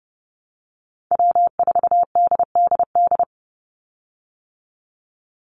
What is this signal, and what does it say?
Morse code, heard as a keyed tone, about 30 wpm, 700 Hz. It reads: W4DDD
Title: Call Signs - 30wpm